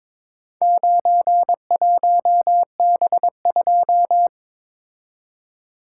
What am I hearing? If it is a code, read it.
Morse code, heard as a keyed tone, about 23 wpm, 700 Hz. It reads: 91B2
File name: practice / 22wpm / Four Letter-Number Combinations - 22wpm